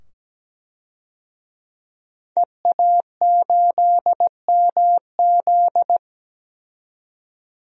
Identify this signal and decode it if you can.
Morse code, heard as a keyed tone, about 17 wpm, 700 Hz. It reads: EA8MZ
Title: DXCC-100 Call Signs - 5 Characters - 17wpm